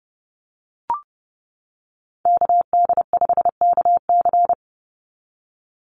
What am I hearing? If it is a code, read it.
Morse code, heard as a keyed tone, about 30 wpm, 700 Hz. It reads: KD5KC
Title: NA SOTA Activator Callsigns 30wpm